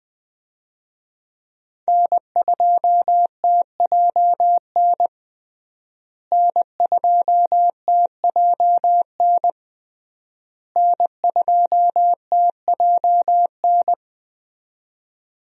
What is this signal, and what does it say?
Morse code, heard as a keyed tone, about 20 wpm, 700 Hz. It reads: N2TJN N2TJN N2TJN
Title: USA Call Signs - 5 Characters - Repeated 3x - 20wpm